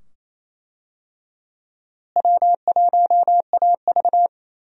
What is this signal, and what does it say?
Morse code, heard as a keyed tone, about 28 wpm, 700 Hz. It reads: W1AV
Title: USA Call Signs - 4 Characters - Repeated 2x - 28wpm